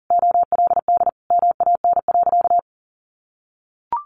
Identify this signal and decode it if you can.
Morse code, heard as a keyed tone, about 40 wpm, 700 Hz. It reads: OLD MAN.
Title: Sentences from Top 1500 Words 40wpm